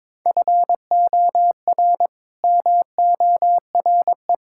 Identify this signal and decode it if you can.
Morse code, heard as a keyed tone, about 22 wpm, 700 Hz. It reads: FOR MORE
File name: practice / 22wpm / Build A Phrase - 5-Word Phrases - 22wpm